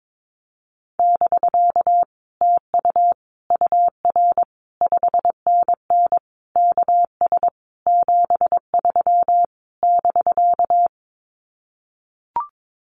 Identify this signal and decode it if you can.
Morse code, heard as a keyed tone, about 22 wpm, 700 Hz. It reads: <BK> TU UR 5NN KS 73 <BK>
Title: Common QSO Phrases Encore 22wpm